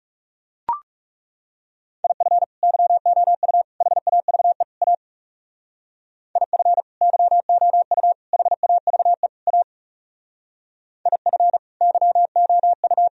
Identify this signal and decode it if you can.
Morse code, heard as a keyed tone, about 37 wpm, 700 Hz. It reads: IF YOU HAVE A IF YOU HAVE A IF YOU
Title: Speed-Racing - Sets of 4 Words 30wpm